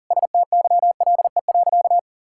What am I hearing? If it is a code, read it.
Morse code, heard as a keyed tone, about 37 wpm, 700 Hz. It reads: STYLE.
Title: Sentences from Top 1000 Words - Review Part 5 - 40wpm